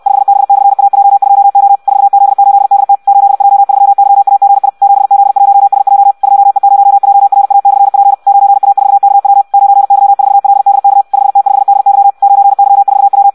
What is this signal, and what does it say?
Morse code, heard as a keyed tone, about 64 wpm, 800 Hz. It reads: 5B/AJ2O 4Z8DT 8J3YAGI 7Z1SJ 2E0JRA/P 9A4KW 9Y4LDK 5T5DY 9Y4W
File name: Audio practica CW / qrq-tx-66 2